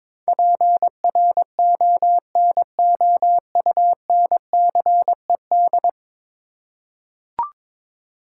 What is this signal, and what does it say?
Morse code, heard as a keyed tone, about 22 wpm, 700 Hz. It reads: PRONOUNCED
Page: morse-code-ninja-practice-22wpm